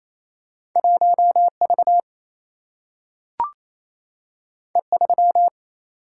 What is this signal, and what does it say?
Morse code, heard as a keyed tone, about 28 wpm, 700 Hz. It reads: 1V E3
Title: Single-Letter Single-Number 28wpm